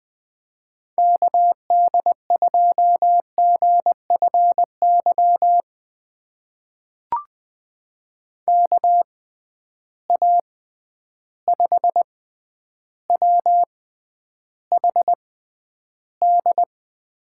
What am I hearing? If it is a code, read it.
Morse code, heard as a keyed tone, about 20 wpm, spaced as for 10 wpm, 700 Hz. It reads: KD2GFY KA5WHD
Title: USA Call Signs - 6 Characters - 6x Character Spacing - 20wpm